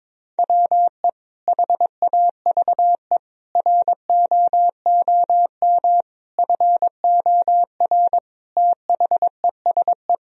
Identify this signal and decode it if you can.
Morse code, heard as a keyed tone, about 22 wpm, 700 Hz. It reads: WE HAVE ROOM FOR THESE
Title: Sentences from Top 1000 Words - Review Part 1 - 22wpm